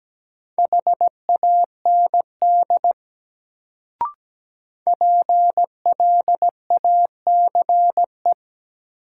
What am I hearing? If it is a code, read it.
Morse code, heard as a keyed tone, about 17 wpm, 700 Hz. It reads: HAND PLACE
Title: Sentences from Top 200 Words - New Word Drill - 17wpm